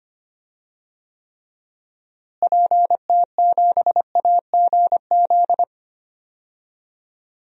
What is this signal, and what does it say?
Morse code, heard as a keyed tone, about 25 wpm, 700 Hz. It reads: PT7AGZ
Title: DXCC-100 Call Signs - 6 Characters - Repeated 3x - 25wpm